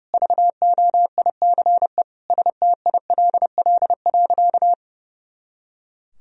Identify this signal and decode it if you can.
Morse code, heard as a keyed tone, about 30 wpm, 700 Hz. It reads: VOICE STILL.
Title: Sentences from Top 500 Words and Abbreviations 30wpm